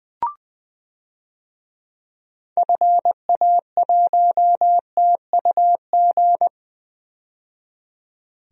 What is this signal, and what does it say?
Morse code, heard as a keyed tone, about 20 wpm, 700 Hz. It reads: FA1TUG
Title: DXCC-100 Call Signs - 6 Characters - 20wpm